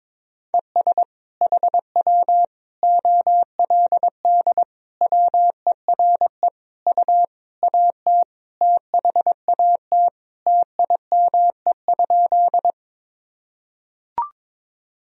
Morse code, heard as a keyed tone, about 22 wpm, 700 Hz. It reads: ES HW OLD WERE U AT THAT TIME?
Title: Sentences from Top 200 Words and Abbreviations 22wpm